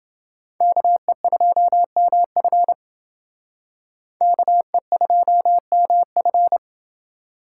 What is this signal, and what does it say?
Morse code, heard as a keyed tone, about 28 wpm, 700 Hz. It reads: KE2MF KE2MF